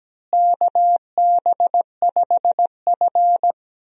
Morse code, heard as a keyed tone, about 17 wpm, 700 Hz. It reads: KB5F
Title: USA Call Signs - 4 Characters - Repeated 3x - 17wpm